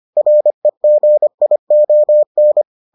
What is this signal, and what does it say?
Morse code, heard as a keyed tone, about 25 wpm, 600 Hz. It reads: REGION